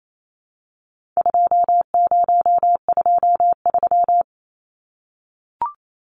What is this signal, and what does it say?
Morse code, heard as a keyed tone, about 28 wpm, 700 Hz. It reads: 2023
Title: Numbers - 4-Digit 28wpm